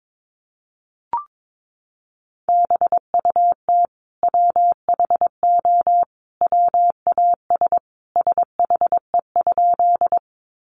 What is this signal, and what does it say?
Morse code, heard as a keyed tone, about 22 wpm, 700 Hz. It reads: BUT WHO WAS SHE?
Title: Sentences from Top 100 Words 22wpm